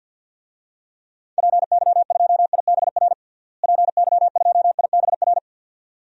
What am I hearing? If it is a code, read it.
Morse code, heard as a keyed tone, about 53 wpm, 700 Hz. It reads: PY2IBR PY2IBR
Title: DXCC-100 Call Signs - 6 Characters - Repeated 2x - 50wpm